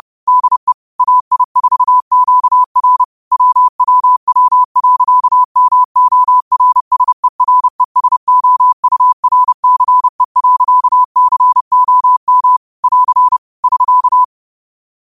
Morse code, heard as a keyed tone, about 30 wpm, 1000 Hz. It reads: DE AI4QR WWW.MORSERESOURCE.COM <AR> <SK>